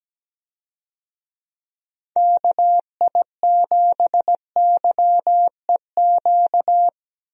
Morse code, heard as a keyed tone, about 17 wpm, 700 Hz. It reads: KI7YEQ
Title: USA Call Signs - 6 Characters - Repeated 3x - 17wpm